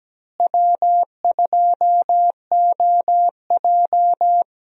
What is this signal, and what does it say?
Morse code, heard as a keyed tone, about 17 wpm, 700 Hz. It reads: W2OJ
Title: USA Call Signs - 4 Characters - Repeated 3x - 17wpm